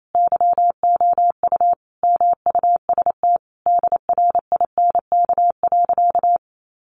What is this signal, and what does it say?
Morse code, heard as a keyed tone, about 28 wpm, 700 Hz. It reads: YOU MUST DRINK.
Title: Sentences from Top 1500 Words 28wpm